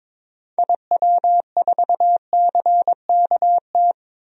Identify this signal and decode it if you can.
Morse code, heard as a keyed tone, about 22 wpm, 700 Hz. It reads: IW4CKT
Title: DXCC-100 Call Signs - 6 Characters - 22wpm